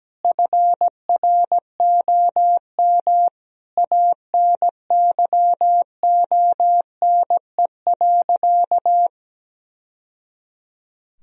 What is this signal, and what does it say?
Morse code, heard as a keyed tone, about 17 wpm, 700 Hz. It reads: FROM ANYONE.
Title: Sentences from Top 1000 Words - Review Part 3 - 17wpm